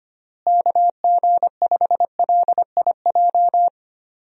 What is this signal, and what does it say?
Morse code, heard as a keyed tone, about 25 wpm, 700 Hz. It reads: KG5LIJ
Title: USA Call Signs - 6 Characters - Repeated 3x - 25wpm